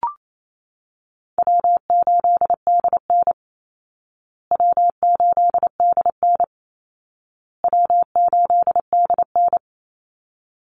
Morse code, heard as a keyed tone, about 28 wpm, 700 Hz. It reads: W8DN W8DN W8DN
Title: Call Signs Repeated 2x - 28wpm